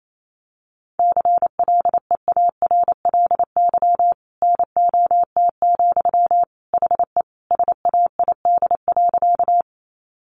Morse code, heard as a keyed tone, about 28 wpm, 700 Hz. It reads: CLEARLY NOT, HE SAID.
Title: Sentences from Top 1300 Words 28wpm